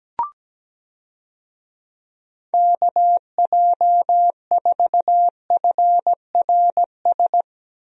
Morse code, heard as a keyed tone, about 17 wpm, 700 Hz. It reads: KJ4FRS
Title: USA Call Signs - 6 Characters - Repeated 3x - 17wpm